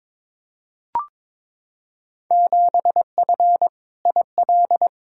Morse code, heard as a keyed tone, about 22 wpm, 700 Hz. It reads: 7F IL
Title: Field Day Exchanges Encore - 22wpm